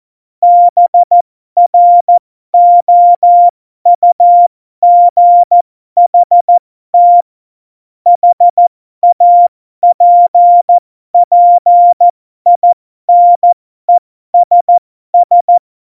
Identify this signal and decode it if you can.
Morse code, heard as a keyed tone, about 14 wpm, spaced as for 12 wpm, 700 Hz. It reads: BROUGHT HAPPINESS
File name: Morsekurs / Witze